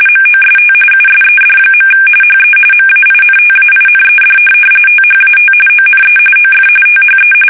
RTTY, about 45.45 baud. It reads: MOST USED. HE MAINTAINED THAT ONE OF THE
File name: fsk850 / benjamin-g-lamme